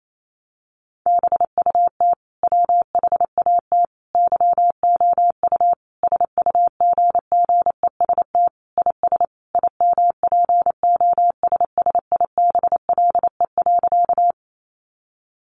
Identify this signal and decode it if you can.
Morse code, heard as a keyed tone, about 28 wpm, 700 Hz. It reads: BUT WHAT YOU SUGGEST IS IMPOSSIBLE.
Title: Sentences from Top 1300 Words 28wpm